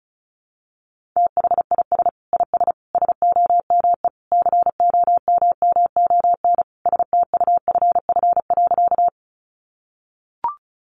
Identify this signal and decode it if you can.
Morse code, heard as a keyed tone, about 34 wpm, 700 Hz. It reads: THIS IS SOME COMMON STUFF.
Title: Sentences from Top 1400 Words 35wpm